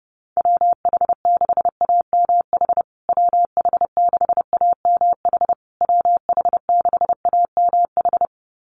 Morse code, heard as a keyed tone, about 30 wpm, 700 Hz. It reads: WH6AMH WH6AMH WH6AMH
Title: USA Call Signs Repeated 3x 30wpm